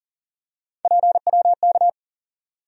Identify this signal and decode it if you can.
Morse code, heard as a keyed tone, about 40 wpm, 700 Hz. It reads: PWK
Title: USA Call Sign Postfixes - 3 Letters 40wpm